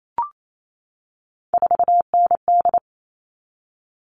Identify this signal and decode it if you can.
Morse code, heard as a keyed tone, about 28 wpm, 700 Hz. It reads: 4ND
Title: All Cut Numbers - 3-Digit 28wpm